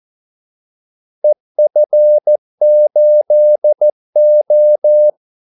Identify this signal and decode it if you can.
Morse code, heard as a keyed tone, about 14 wpm, 600 Hz. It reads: EF8O
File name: Callsigns-14wpm